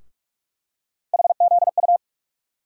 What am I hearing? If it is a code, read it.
Morse code, heard as a keyed tone, about 42 wpm, 700 Hz. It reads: HZU
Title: USA Call Sign Postfixes - 3 Letters 45wpm